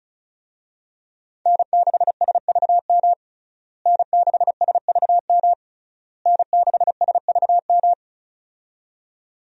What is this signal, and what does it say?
Morse code, heard as a keyed tone, about 36 wpm, 700 Hz. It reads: N6SVM N6SVM N6SVM
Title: USA Call Signs - 5 Characters - Repeated 3x - 35wpm